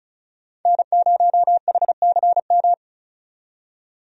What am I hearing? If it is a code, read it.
Morse code, heard as a keyed tone, about 36 wpm, 700 Hz. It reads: N0HCM
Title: USA Call Signs - 5 Characters - 35wpm